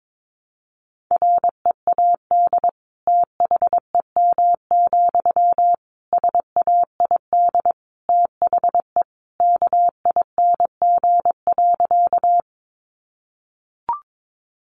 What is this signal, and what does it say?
Morse code, heard as a keyed tone, about 22 wpm, 700 Hz. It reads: READ THEM, SAID THE KING.
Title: Sentences from Top 400 Words 22wpm